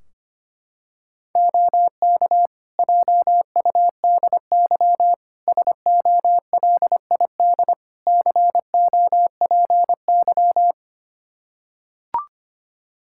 Morse code, heard as a keyed tone, about 25 wpm, 700 Hz. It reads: OK JUDY SOLID COPY
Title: Practice QSO - Exchange 3 - More Info - 25wpm